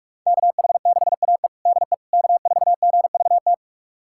Morse code, heard as a keyed tone, about 45 wpm, 700 Hz. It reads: KH6AE DE K4GVT
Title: USA Callsign DE Callsign - 5 Characters - 45wpm